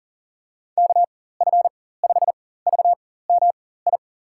Morse code, heard as a keyed tone, about 40 wpm, spaced as for 19 wpm, 700 Hz. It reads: KF5VMI
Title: USA Call Signs - 6 Characters - 4x Character Spacing - 40wpm